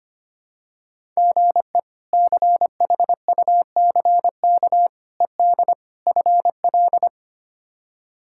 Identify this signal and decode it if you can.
Morse code, heard as a keyed tone, about 25 wpm, 700 Hz. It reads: GE CHUCK ED FL
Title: K1USN SST QSO Practice - 25wpm